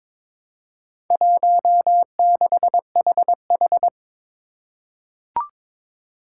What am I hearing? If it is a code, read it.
Morse code, heard as a keyed tone, about 22 wpm, 700 Hz. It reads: 16HH
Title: Four Challenging Letter-Number Combinations 22wpm